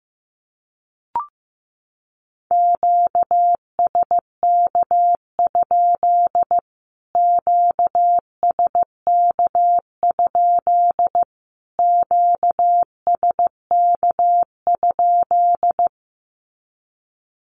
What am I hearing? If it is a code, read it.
Morse code, heard as a keyed tone, about 14 wpm, 700 Hz. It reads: QSK? QSK? QSK?
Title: Q-Codes Repeated 3x 15wpm